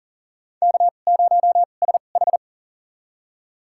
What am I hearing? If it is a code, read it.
Morse code, heard as a keyed tone, about 40 wpm, spaced as for 29 wpm, 700 Hz. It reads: K0SH
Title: USA Call Signs - 4 Characters - 2x Character Spacing - 40wpm